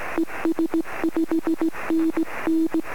Morse code, heard as a keyed tone, about 16 wpm, 330 Hz. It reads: ES 5NN
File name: sm7lcb / 21-sm6qa-23cm